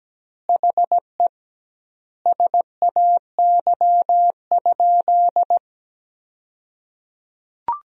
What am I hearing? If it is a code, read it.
Morse code, heard as a keyed tone, about 17 wpm, 700 Hz. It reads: HE SAY?
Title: 2x Word-Spacing - Sentences from Top 200 Words - 17wpm